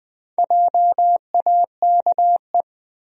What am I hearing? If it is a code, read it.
Morse code, heard as a keyed tone, about 20 wpm, 700 Hz. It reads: JAKE
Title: USA Common Names - Max 4 Letters - 20wpm